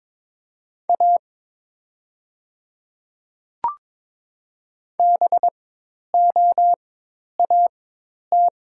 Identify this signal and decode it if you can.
Morse code, heard as a keyed tone, about 22 wpm, spaced as for 10 wpm, 700 Hz. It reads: A BOAT
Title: 4x Character Spacing - Top 500 Words - 22wpm